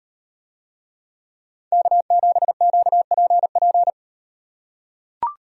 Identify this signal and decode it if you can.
Morse code, heard as a keyed tone, about 38 wpm, 700 Hz. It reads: K7QPP